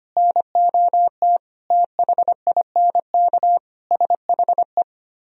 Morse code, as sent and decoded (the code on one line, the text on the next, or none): -. --- - / - .... .. -. -.- / ... .... .
NOT THINK SHE